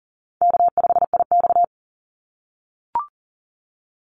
Morse code, heard as a keyed tone, about 42 wpm, 700 Hz. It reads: K5IX
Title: Easy Call Signs - 40wpm